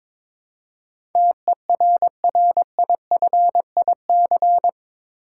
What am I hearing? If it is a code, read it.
Morse code, heard as a keyed tone, about 22 wpm, 700 Hz. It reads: TERRIFIC
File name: practice / 22wpm / Eight Letter Words - 22wpm